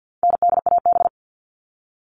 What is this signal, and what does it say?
Morse code, heard as a keyed tone, about 50 wpm, 700 Hz. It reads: NDAB
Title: All Cut Numbers - 4-Digit 50wpm